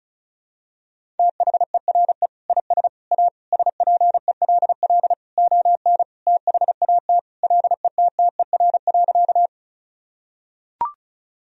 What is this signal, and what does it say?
Morse code, heard as a keyed tone, about 36 wpm, 700 Hz. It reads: THERE IS A SPELL ON THAT LETTER.
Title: Sentences from Top 1000 Words - Review Part 1 - 35wpm